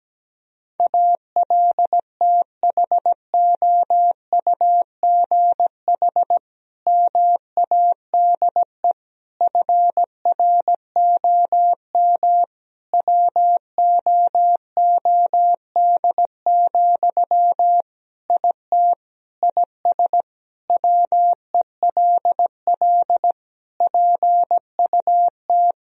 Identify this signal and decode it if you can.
Morse code, heard as a keyed tone, about 17 wpm, 700 Hz. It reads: ALTHOUGH MADE FROM WOOD, IT IS WELL PUT
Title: Sentences from Top 1000 Words - Review Part 3 - 17wpm